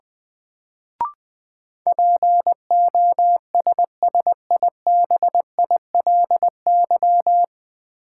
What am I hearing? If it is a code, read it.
Morse code, heard as a keyed tone, about 20 wpm, 700 Hz. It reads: POSSIBILY
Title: Sentences from Top 1300 Words - New Word Drill Alt - 20wpm